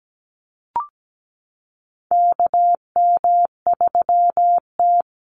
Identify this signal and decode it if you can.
Morse code, heard as a keyed tone, about 17 wpm, 700 Hz. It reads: KM3T
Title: Call Signs Repeated 2x - 17wpm